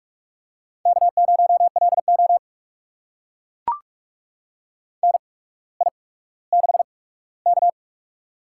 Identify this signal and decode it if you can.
Morse code, heard as a keyed tone, about 44 wpm, spaced as for 19 wpm, 700 Hz. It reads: K0LO NI6K
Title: USA Call Signs - 4 Characters - 8x Character Spacing - 45wpm